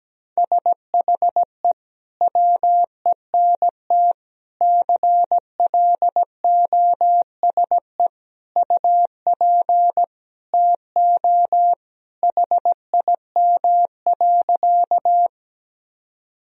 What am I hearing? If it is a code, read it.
Morse code, heard as a keyed tone, about 17 wpm, 700 Hz. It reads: SHE WENT CLOSE UP TO HIM.
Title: Sentences from Top 1000 Words - Review Part 1 - 17wpm